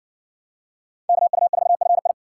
Morse code, heard as a keyed tone, about 66 wpm, 700 Hz. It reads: KU4FI